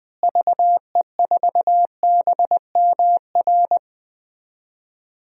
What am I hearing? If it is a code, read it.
Morse code, heard as a keyed tone, about 20 wpm, 700 Hz. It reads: VE4BMR
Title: DXCC-100 Call Signs - 6 Characters - Repeated 3x - 20wpm